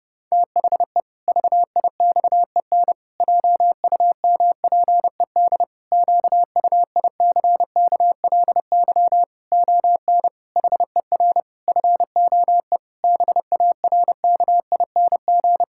Morse code, heard as a keyed tone, about 30 wpm, 700 Hz. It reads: THE VIXEN JUMPED QUICKLY ON HER FOE BARKING
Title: Pangram Challenge v2 - 30wpm